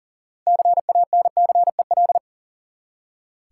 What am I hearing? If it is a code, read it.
Morse code, heard as a keyed tone, about 40 wpm, 700 Hz. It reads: CANCEL
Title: Six Letter Words - 40wpm